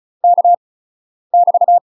Morse code, heard as a keyed tone, about 36 wpm, 700 Hz. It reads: K <BT>
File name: UK MASTER QSOs TRAINING / 35WPM / UK MASTER QSOs 0010